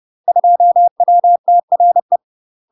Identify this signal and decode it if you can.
Morse code, heard as a keyed tone, about 30 wpm, 700 Hz. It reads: 2WTRE
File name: UK MASTER QSOs TRAINING / 30WPM / UK MASTER QSOs 0049